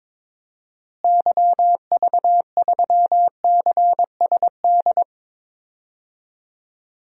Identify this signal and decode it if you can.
Morse code, heard as a keyed tone, about 22 wpm, 700 Hz. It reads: YV3CSD
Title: DXCC-100 Call Signs - 6 Characters - Repeated 2x - 22wpm